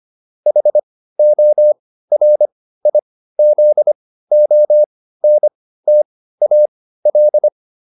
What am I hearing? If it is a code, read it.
Morse code, heard as a keyed tone, about 25 wpm, spaced as for 15 wpm, 600 Hz. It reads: HORIZONTAL